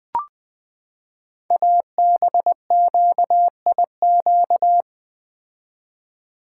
Morse code, heard as a keyed tone, about 20 wpm, 700 Hz. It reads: ABQIQ
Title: Code Groups - 5 Letters - 20wpm